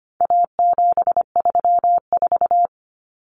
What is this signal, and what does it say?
Morse code, heard as a keyed tone, about 25 wpm, 700 Hz. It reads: A734
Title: Common Cut Numbers - 4-Digit 25wpm